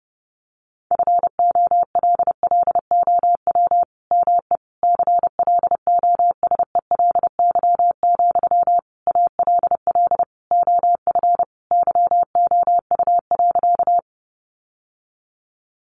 Morse code, heard as a keyed tone, about 30 wpm, 700 Hz. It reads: FOLLOW ME CLOSELY, ALL OF YOU.
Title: Sentences from Top 1400 Words 30wpm